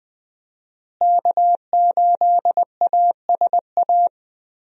Morse code, heard as a keyed tone, about 20 wpm, 700 Hz. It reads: K8ASA
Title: USA Call Signs - 5 Characters - Repeated 3x - 20wpm